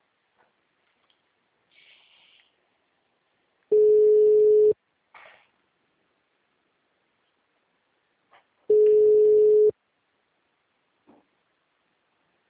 Ringback tone (Europe, 425 Hz).
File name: out- -2032- - - .28